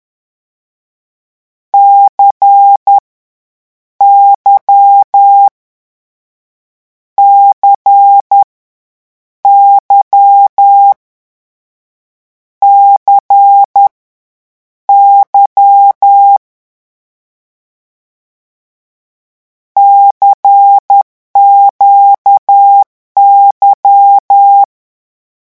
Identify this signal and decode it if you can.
Morse code, heard as a keyed tone, about 10 wpm, 800 Hz. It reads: C Y C Y C Y CQY
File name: morse mp3 / lecon2-vitesse 10